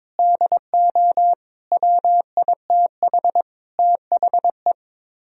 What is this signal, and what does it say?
Morse code, heard as a keyed tone, about 23 wpm, 700 Hz. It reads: DO WITH THE
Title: Speed-Racing - Sets of 5 Words 22wpm